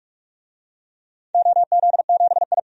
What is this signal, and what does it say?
Morse code, heard as a keyed tone, about 44 wpm, 700 Hz. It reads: OZ7I
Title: DXCC-100 Call Signs - 4 Characters - 45wpm